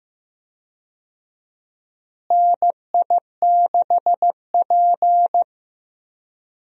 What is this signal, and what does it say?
Morse code, heard as a keyed tone, about 15 wpm, 700 Hz. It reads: NI6P
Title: USA Call Signs - 4 Characters - Repeated 2x - 15wpm